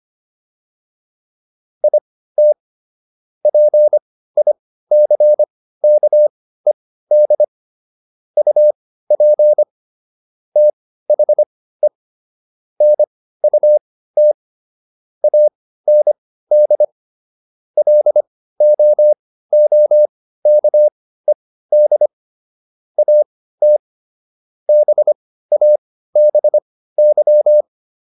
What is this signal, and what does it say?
Morse code, heard as a keyed tone, about 25 wpm, spaced as for 15 wpm, 600 Hz. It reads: IT PICKED UP THE NUT AND LOOKED AT BABY